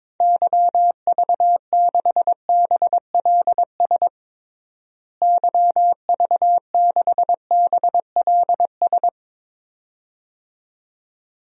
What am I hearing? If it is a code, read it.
Morse code, heard as a keyed tone, about 21 wpm, 700 Hz. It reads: YV6BLS YV6BLS
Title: DXCC-100 Call Signs - 6 Characters - Repeated 2x - 22wpm